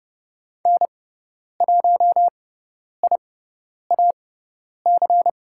Morse code, heard as a keyed tone, about 30 wpm, spaced as for 10 wpm, 700 Hz. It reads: N1IAC
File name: Lesson 17g - Letter u - Call Signs 30wpm